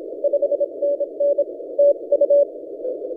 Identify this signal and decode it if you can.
Morse code, heard as a keyed tone, about 26 wpm, 570 Hz. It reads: 5NN TU
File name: HP0INT/1 NA-072 @Panama IOTAs 17 CW